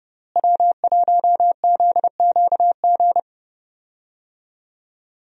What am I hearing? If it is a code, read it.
Morse code, heard as a keyed tone, about 29 wpm, 700 Hz. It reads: W1ZQG
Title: USA Call Signs - 5 Characters - Repeated 3x - 30wpm